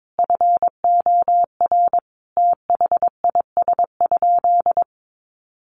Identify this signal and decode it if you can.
Morse code, heard as a keyed tone, about 22 wpm, 700 Hz. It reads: FOR THIS?
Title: Sentences from Top 1300 Words 22wpm